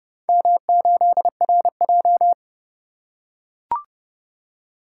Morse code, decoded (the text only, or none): M8RJ